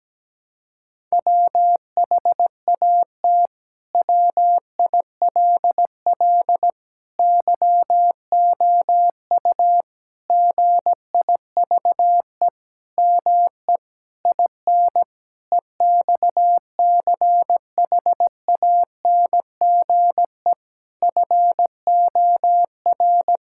Morse code, heard as a keyed tone, about 17 wpm, 700 Hz. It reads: WHAT WILL YOU GIVE ME IN EXCHANGE FOR
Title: Sentences from Top 1000 Words - Review Part 6 - 17wpm